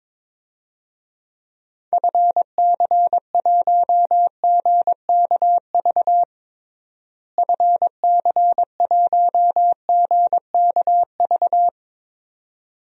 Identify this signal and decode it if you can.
Morse code, heard as a keyed tone, about 22 wpm, 700 Hz. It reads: FC1GKV FC1GKV
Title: DXCC-100 Call Signs - 6 Characters - Repeated 2x - 22wpm